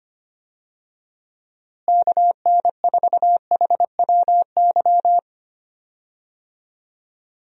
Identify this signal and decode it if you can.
Morse code, heard as a keyed tone, about 25 wpm, 700 Hz. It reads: KN4HWY
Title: USA Call Signs - 6 Characters - Repeated 3x - 25wpm